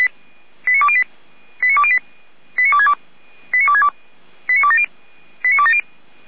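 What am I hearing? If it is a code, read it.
Five-tone selective call (ZVEI), heard as quick runs of five tones, tones of about 70 ms. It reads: 89208 89208 89262 89262 89270 89270
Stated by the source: ZVEI five-tone selective call 89340 received by SDR.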